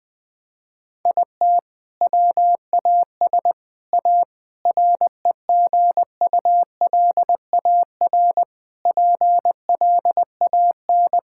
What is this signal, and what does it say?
Morse code, heard as a keyed tone, about 20 wpm, 700 Hz. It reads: IT WAS A REGULAR PLAN
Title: Sentences from Top 1000 Words - Review Part 6 - 20wpm